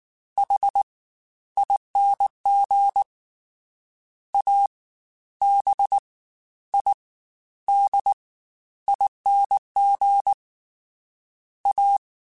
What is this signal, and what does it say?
Morse code, heard as a keyed tone, about 19 wpm, 800 Hz. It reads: H ING A B I D ING A